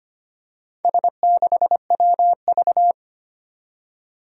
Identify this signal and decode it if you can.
Morse code, heard as a keyed tone, about 25 wpm, 700 Hz. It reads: S6WV
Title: Four Challenging Letter-Number Combinations 25wpm